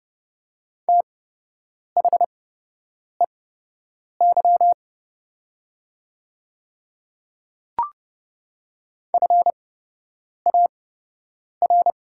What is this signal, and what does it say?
Morse code, heard as a keyed tone, about 30 wpm, spaced as for 8 wpm, 700 Hz. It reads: THEY FAR
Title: 8x Character Spacing - Top 300 Words - 30wpm